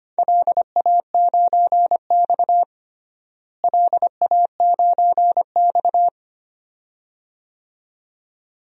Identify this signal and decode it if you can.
Morse code, heard as a keyed tone, about 25 wpm, 700 Hz. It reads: LA9X LA9X
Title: DXCC-100 Call Signs - 4 Characters - Repeated 2x - 25wpm